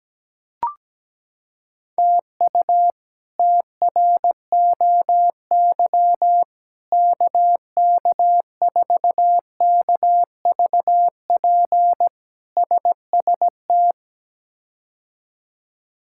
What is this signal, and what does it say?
Morse code, heard as a keyed tone, about 17 wpm, 700 Hz. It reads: TU TROY KK4KVP SST
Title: K1USN SST QSO Practice - 17wpm